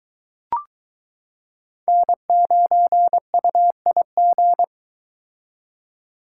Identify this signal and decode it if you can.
Morse code, heard as a keyed tone, about 22 wpm, 700 Hz. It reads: N9UIG